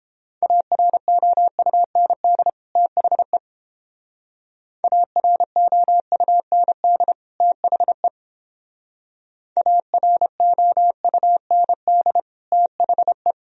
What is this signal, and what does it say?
Morse code, heard as a keyed tone, about 29 wpm, 700 Hz. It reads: AROUND THE AROUND THE AROUND THE